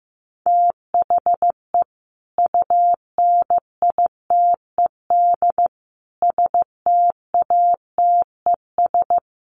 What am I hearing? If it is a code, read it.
Morse code, heard as a keyed tone, about 15 wpm, 700 Hz. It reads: THE UNITED STATES
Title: Sets of 4 Words 15wpm